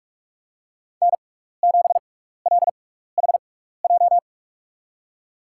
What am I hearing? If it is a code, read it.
Morse code, heard as a keyed tone, about 47 wpm, spaced as for 16 wpm, 700 Hz. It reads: N7LHJ
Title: USA Call Signs - 5 Characters - 6x Character Spacing - 45wpm